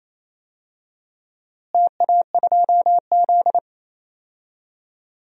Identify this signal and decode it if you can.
Morse code, heard as a keyed tone, about 27 wpm, 700 Hz. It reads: TA2Z
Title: DXCC-100 Call Signs - 4 Characters - Repeated 3x - 28wpm